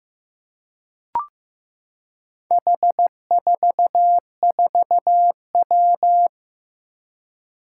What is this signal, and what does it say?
Morse code, heard as a keyed tone, about 15 wpm, 700 Hz. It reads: H44W
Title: Four Challenging Letter-Number Combinations 15wpm